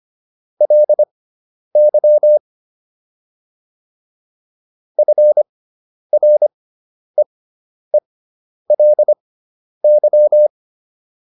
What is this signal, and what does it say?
Morse code, heard as a keyed tone, about 25 wpm, spaced as for 10 wpm, 600 Hz. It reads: LY FREELY